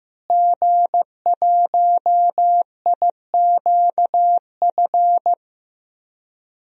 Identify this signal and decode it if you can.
Morse code, heard as a keyed tone, about 16 wpm, 700 Hz. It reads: G1IQF
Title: DXCC-100 Call Signs - 5 Characters - Repeated 2x - 15wpm